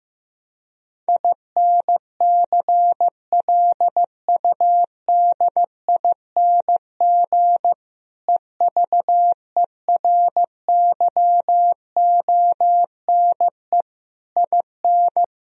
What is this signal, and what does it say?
Morse code, heard as a keyed tone, about 15 wpm, 700 Hz. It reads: INCLUDING EVERYONE IN
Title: Sentences from Top 1000 Words - Review Part 4 - 15wpm